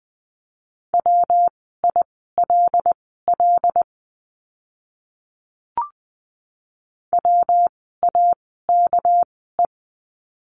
Morse code, heard as a keyed tone, about 20 wpm, spaced as for 14 wpm, 700 Hz. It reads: WILL WAKE
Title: 199 Lesson 32f - Letter k - Words 20wpm